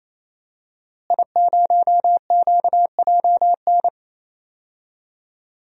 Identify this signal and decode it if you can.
Morse code, heard as a keyed tone, about 28 wpm, 700 Hz. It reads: I0QJN
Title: DXCC-100 Call Signs - 5 Characters - Repeated 3x - 28wpm